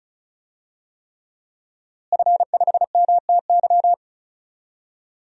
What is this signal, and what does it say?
Morse code, heard as a keyed tone, about 36 wpm, 700 Hz. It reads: F5MTY
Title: DXCC-100 Call Signs - 5 Characters - Repeated 3x - 35wpm